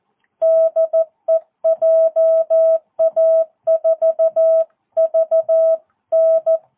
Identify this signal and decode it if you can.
Morse code, heard as a keyed tone, about 14 wpm, 650 Hz. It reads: DE JA4VN
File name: jh8cbh / 2022cw